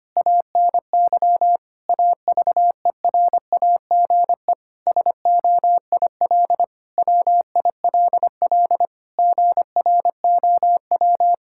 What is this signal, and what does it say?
Morse code, heard as a keyed tone, about 25 wpm, 700 Hz. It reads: ANY AVERAGE SOIL WILL GROW
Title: Sentences from Top 1000 Words - Review Part 6 - 25wpm